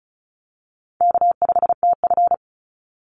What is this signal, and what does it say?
Morse code, heard as a keyed tone, about 34 wpm, 700 Hz. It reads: K5TF
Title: Easy Call Signs Repeated 2x - 35wpm